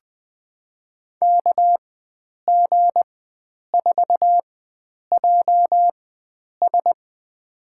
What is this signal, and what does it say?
Morse code, heard as a keyed tone, about 20 wpm, spaced as for 9 wpm, 700 Hz. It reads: KG4JS
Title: USA Call Signs - 4x Character Spacing - 20wpm